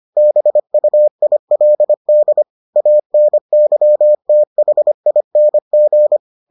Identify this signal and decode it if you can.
Morse code, heard as a keyed tone, about 25 wpm, 600 Hz. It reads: BUILD ANYTHING